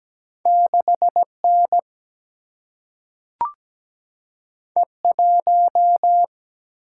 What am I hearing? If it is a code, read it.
Morse code, heard as a keyed tone, about 17 wpm, 700 Hz. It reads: 6N E1
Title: Single-Letter Single-Number 17wpm